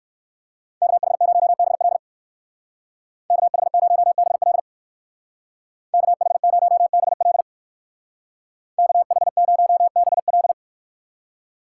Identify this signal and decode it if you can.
Morse code, heard as a keyed tone, about 54 wpm, 700 Hz. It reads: KH0BL KH0BL KH0BL KH0BL
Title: Speed Racing - USA 2x2 Call Signs 45wpm